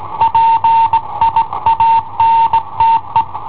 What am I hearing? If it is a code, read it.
Morse code, heard as a keyed tone, about 19 wpm, 910 Hz. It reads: PIANTE
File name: i7ohp / cassetta i7ohp B